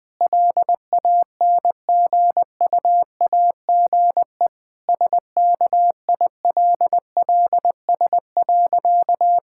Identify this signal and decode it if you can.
Morse code, heard as a keyed tone, about 20 wpm, 700 Hz. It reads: LANGUAGE SKILLS.
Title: Difficult Sentences with Hints - 20wpm